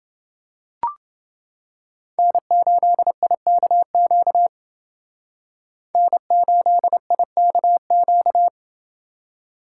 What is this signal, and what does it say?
Morse code, heard as a keyed tone, about 30 wpm, 700 Hz. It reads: N8IKQ N8IKQ